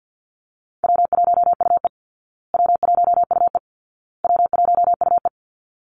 Morse code, heard as a keyed tone, about 50 wpm, 700 Hz. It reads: W1UE W1UE W1UE
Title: Easy Call Signs Repeated 3x - 50wpm